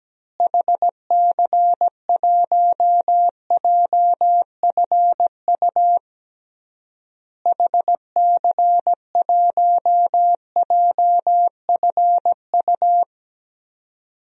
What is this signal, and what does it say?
Morse code, heard as a keyed tone, about 17 wpm, 700 Hz. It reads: HC1JFU HC1JFU
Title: DXCC-100 Call Signs - 6 Characters - Repeated 3x - 17wpm